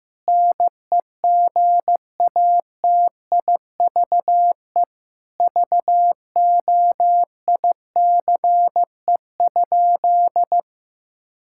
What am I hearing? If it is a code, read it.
Morse code, heard as a keyed tone, about 15 wpm, 700 Hz. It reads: NEGATIVE VOICE?
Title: Sentences from Top 1000 Words - Review Part 6 - 15wpm